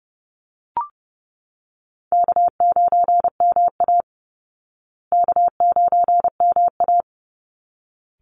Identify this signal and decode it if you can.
Morse code, heard as a keyed tone, about 30 wpm, 700 Hz. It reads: K9MA K9MA
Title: Call Signs Repeated 3x - 30wpm